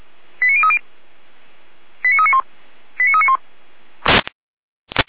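Five-tone selective call (ZVEI), heard as quick runs of five tones, tones of about 70 ms. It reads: 89030 89391 89391
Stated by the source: ZVEI five-tone selective call received on SDR.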